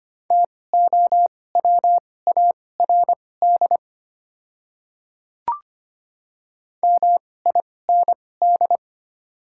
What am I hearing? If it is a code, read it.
Morse code, heard as a keyed tone, about 25 wpm, spaced as for 18 wpm, 690 Hz. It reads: TOWARD MIND
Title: 2x Character Spacing - Top 500 Words - 25wpm